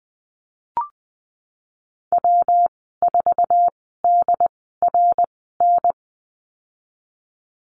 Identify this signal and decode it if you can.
Morse code, heard as a keyed tone, about 20 wpm, spaced as for 15 wpm, 700 Hz. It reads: W4DRN
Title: 167 Lesson 27g - Number 7 - Call Signs 20wpm